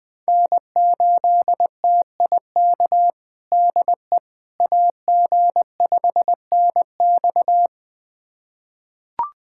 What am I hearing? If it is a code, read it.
Morse code, heard as a keyed tone, about 20 wpm, 700 Hz. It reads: N8TIK DE AG5NX
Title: USA Callsign DE Callsign - 5 Characters - 20wpm